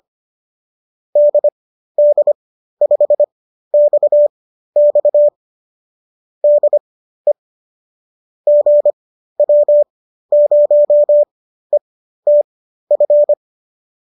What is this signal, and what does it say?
Morse code, heard as a keyed tone, about 25 wpm, spaced as for 13 wpm, 590 Hz. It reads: DD5XX DE GW0ETF